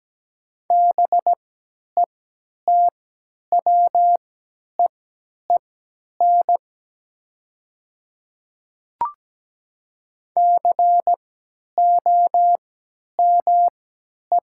Morse code, heard as a keyed tone, about 17 wpm, spaced as for 10 wpm, 700 Hz. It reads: BETWEEN COME
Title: 3x Character Spacing - Top 500 Words - 17wpm